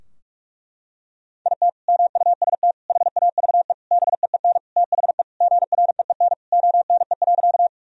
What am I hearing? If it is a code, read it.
Morse code, heard as a keyed tone, about 41 wpm, 700 Hz. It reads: IT MUST HAVE BEEN THE GREEN ONE.
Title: Sentences from Top 1000 Words - Review Part 2 - 45wpm